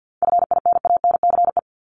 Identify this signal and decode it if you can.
Morse code, heard as a keyed tone, about 50 wpm, 700 Hz. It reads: FINANCE
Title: Top 1251-1500 Words 50wpm